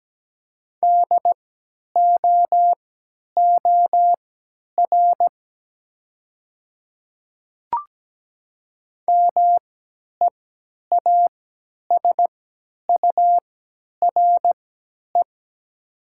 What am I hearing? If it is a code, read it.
Morse code, heard as a keyed tone, about 17 wpm, spaced as for 10 wpm, 700 Hz. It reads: DOOR MEASURE
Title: 3x Character Spacing - Top 500 Words - 17wpm